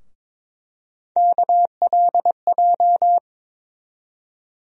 Morse code, heard as a keyed tone, about 21 wpm, 700 Hz. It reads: KLJ
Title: USA Call Sign Postfixes - 3 Letters 22wpm